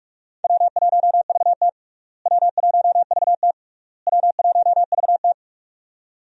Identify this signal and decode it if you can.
Morse code, heard as a keyed tone, about 41 wpm, 700 Hz. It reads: W1VT W1VT W1VT
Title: USA Call Signs - 4 Characters - Repeated 3x - 45wpm